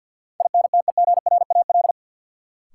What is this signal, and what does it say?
Morse code, heard as a keyed tone, about 53 wpm, 700 Hz. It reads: INTEGRAL
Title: Sets of 5 Words - New Word Drill - 50wpm